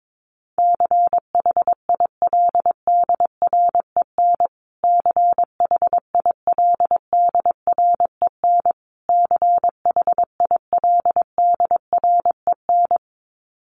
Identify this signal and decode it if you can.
Morse code, heard as a keyed tone, about 22 wpm, 700 Hz. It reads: CHILDREN CHILDREN CHILDREN
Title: Top 300 Words Repeated 22wpm